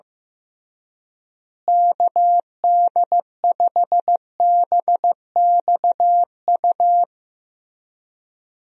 Morse code, heard as a keyed tone, about 15 wpm, 700 Hz. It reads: KD5BXU
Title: USA Call Signs - 6 Characters - Repeated 3x - 15wpm